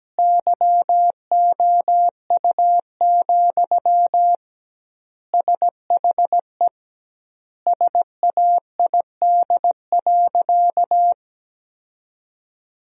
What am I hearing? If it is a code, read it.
Morse code, heard as a keyed tone, about 17 wpm, 700 Hz. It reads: YOU, SHE SAID.
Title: 2x Word-Spacing - Sentences from Top 400 Words - 17wpm